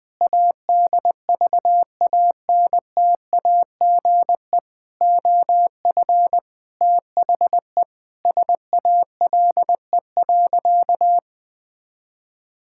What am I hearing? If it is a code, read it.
Morse code, heard as a keyed tone, about 20 wpm, 700 Hz. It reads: ADVANTAGE OF THE SALE.
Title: Sentences with 5-Word Phrases - Spoken Beginning - 20wpm